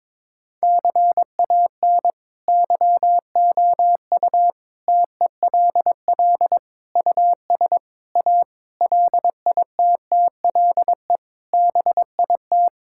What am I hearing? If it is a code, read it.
Morse code, heard as a keyed tone, about 22 wpm, 700 Hz. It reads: CAN YOU TELL US A LITTLE BIT
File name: practice / 22wpm / Sets of 8 Words - 22wpm